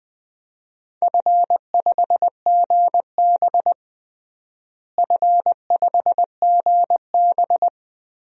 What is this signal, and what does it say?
Morse code, heard as a keyed tone, about 19 wpm, 690 Hz. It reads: F5GB F5GB
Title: DXCC-100 Call Signs - 4 Characters - Repeated 3x - 20wpm